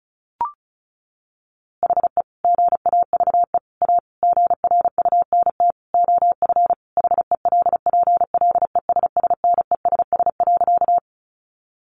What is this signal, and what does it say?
Morse code, heard as a keyed tone, about 34 wpm, 700 Hz. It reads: HE GAVE A GRUNT OF HELPLESSNESS.
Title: A Sound Of Thunder 35wpm